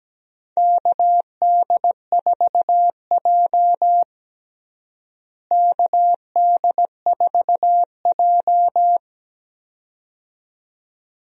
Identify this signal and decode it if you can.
Morse code, heard as a keyed tone, about 17 wpm, 700 Hz. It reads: KD4J KD4J
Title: USA Call Signs - 4 Characters - Repeated 2x - 17wpm